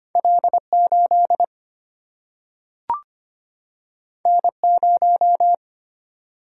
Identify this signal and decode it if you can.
Morse code, heard as a keyed tone, about 25 wpm, 700 Hz. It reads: L8 N0
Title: Single-Letter Single-Number 25wpm